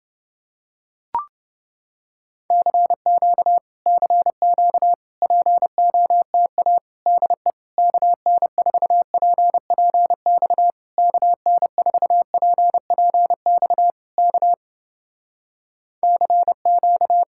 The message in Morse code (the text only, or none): CQ CQ POTA DE KN4PPX KN4PPX K CQ